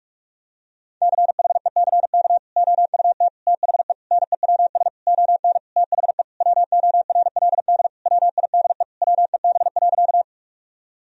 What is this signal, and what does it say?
Morse code, heard as a keyed tone, about 48 wpm, 700 Hz. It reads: CHECK OUT THE NEWS ON THE WORLD WIDE WEB.
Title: Sentences from Top 1000 Words - Review Part 4 - 45wpm